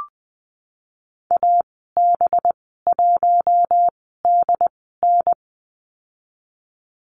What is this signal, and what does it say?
Morse code, heard as a keyed tone, about 20 wpm, spaced as for 15 wpm, 700 Hz. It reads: AB1DN
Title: 213 Lesson 34g - Number 8 - Call Signs 20wpm